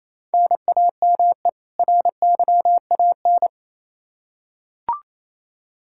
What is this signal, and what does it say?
Morse code, heard as a keyed tone, about 28 wpm, 700 Hz. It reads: NAME RYAN
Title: K1USN SST QSO Practice 28wpm